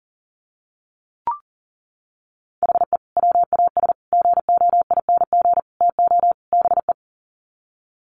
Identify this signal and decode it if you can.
Morse code, heard as a keyed tone, about 40 wpm, 700 Hz. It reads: HE WAS GOING TO BE
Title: Sets of 5 Words 40wpm